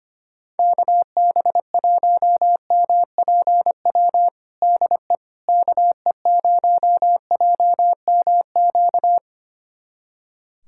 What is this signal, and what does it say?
Morse code, heard as a keyed tone, about 25 wpm, 700 Hz. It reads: KB1MPW DE KE0JMQ
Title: USA Callsign DE Callsign - 6 Characters - 25wpm